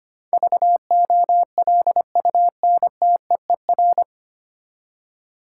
Morse code, heard as a keyed tone, about 25 wpm, 700 Hz. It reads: VOLUNTEER
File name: Nine Letter Words - 25wpm